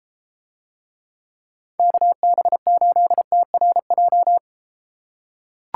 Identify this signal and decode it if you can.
Morse code, heard as a keyed tone, about 32 wpm, 700 Hz. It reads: KB8TRJ